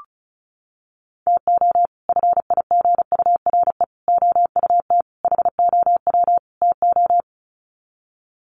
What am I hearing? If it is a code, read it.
Morse code, heard as a keyed tone, about 34 wpm, 700 Hz. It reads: TO FIGURE OUT HOW TO
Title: Sets of 5 Words 35wpm